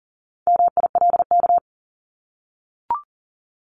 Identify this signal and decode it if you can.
Morse code, heard as a keyed tone, about 40 wpm, 700 Hz. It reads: MILK
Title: Top 1251-1500 Words 40wpm